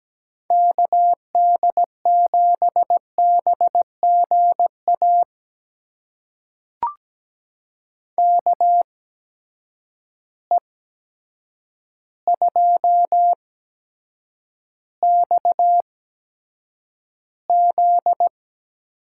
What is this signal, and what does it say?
Morse code, heard as a keyed tone, about 17 wpm, spaced as for 8 wpm, 700 Hz. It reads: KD7BGA KE2XZ
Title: USA Call Signs - 8x Character Spacing - 17wpm